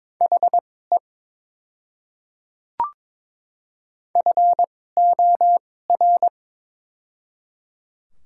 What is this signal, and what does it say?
Morse code, heard as a keyed tone, about 21 wpm, spaced as for 16 wpm, 700 Hz. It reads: HE FOR
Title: 2x Character Spacing - Top 100 Words - 22wpm